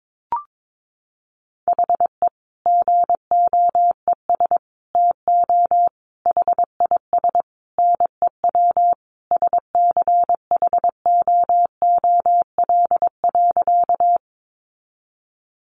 Morse code, heard as a keyed tone, about 22 wpm, 700 Hz. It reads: HE GOES TO HIS NEW SCHOOL.
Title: Easy Phrases 22wpm